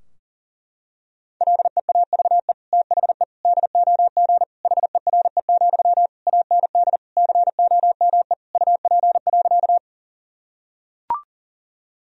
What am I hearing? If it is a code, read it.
Morse code, heard as a keyed tone, about 40 wpm, 700 Hz. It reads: LEAVE THE DOG HERE, AND COME UP.
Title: Sentences from Top 1000 Words - Review Part 2 - 40wpm